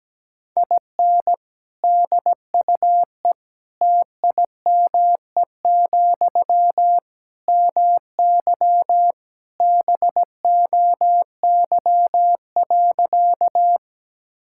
Morse code, heard as a keyed tone, about 17 wpm, 700 Hz. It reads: IN DUE TIME, MY BOY.
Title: Sentences from Top 1000 Words - Review Part 6 - 17wpm